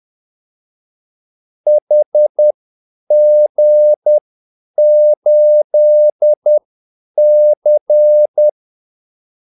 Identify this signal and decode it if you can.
Morse code, heard as a keyed tone, about 10 wpm, spaced as for 8 wpm, 600 Hz. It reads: HG8C
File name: Callsigns-10-8wpm